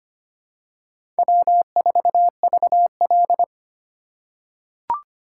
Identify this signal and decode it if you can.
Morse code, heard as a keyed tone, about 25 wpm, 700 Hz. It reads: W4VL
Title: USA Call Signs - 4 Characters - 4x Character Spacing - 25wpm